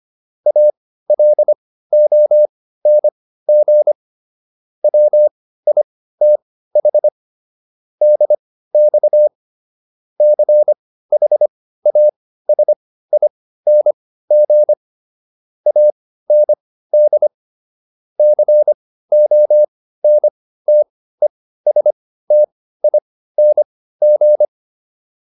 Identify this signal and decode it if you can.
Morse code, heard as a keyed tone, about 25 wpm, spaced as for 15 wpm, 600 Hz. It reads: ALONG WITH DX CHASING AND CONTESTING